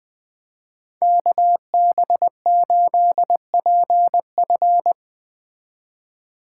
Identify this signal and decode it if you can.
Morse code, heard as a keyed tone, about 21 wpm, 700 Hz. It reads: KB8PF